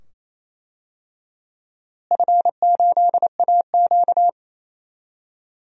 Morse code, heard as a keyed tone, about 28 wpm, 700 Hz. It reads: F8AQ
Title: DXCC-100 Call Signs - 4 Characters - Repeated 2x - 28wpm